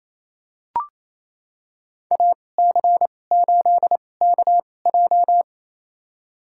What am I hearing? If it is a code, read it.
Morse code, heard as a keyed tone, about 28 wpm, spaced as for 20 wpm, 700 Hz. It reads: AC8KJ
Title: USA Call Signs - 5 Characters - 2x Character Spacing - 28wpm